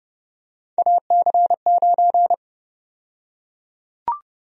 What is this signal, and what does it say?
Morse code, heard as a keyed tone, about 30 wpm, 700 Hz. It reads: AC9
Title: USA Call Sign Prefixes - Repeated 2x - 30wpm